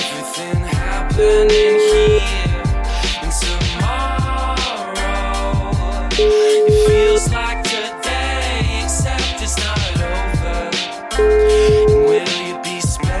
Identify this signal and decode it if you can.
Ringback tone (Europe, 425 Hz), with noise.